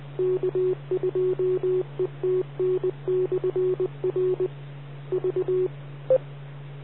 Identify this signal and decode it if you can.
Morse code, heard as a keyed tone, about 20 wpm, 370 Hz. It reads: K2ETN/R V